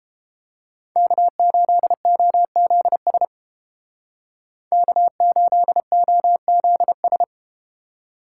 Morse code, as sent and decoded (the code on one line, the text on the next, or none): -.- ---.. --- --.. ... / -.- ---.. --- --.. ...
K8OZS K8OZS